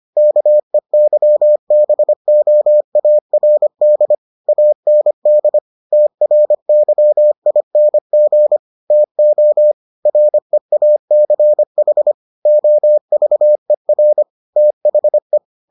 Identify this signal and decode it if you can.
Morse code, heard as a keyed tone, about 25 wpm, 600 Hz. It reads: KEYBOARD AND TRYING TO REACH OVER THE